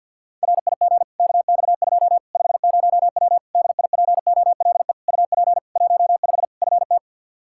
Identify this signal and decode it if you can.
Morse code, heard as a keyed tone, about 53 wpm, 700 Hz. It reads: RIG KX2 50W DIPOLE UP 15 FT
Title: K1USN SST QSO Practice 50wpm